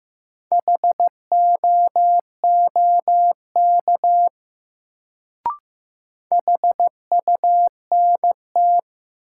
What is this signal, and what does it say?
Morse code, heard as a keyed tone, about 15 wpm, 700 Hz. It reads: HOOK HUNT
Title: Sentences from Top 1600 Words - New Word Drill - 15wpm